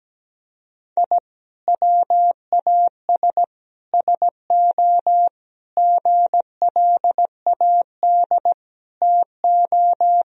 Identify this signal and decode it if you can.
Morse code, heard as a keyed tone, about 17 wpm, 700 Hz. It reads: I WAS SO GLAD TO
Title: Sentences from Top 1000 Words - Review Part 4 - 17wpm